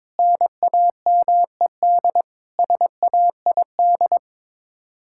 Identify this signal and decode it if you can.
Morse code, heard as a keyed tone, about 22 wpm, 700 Hz. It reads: NAMED SAID
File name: practice / 22wpm / Sets of 7 Words - 22wpm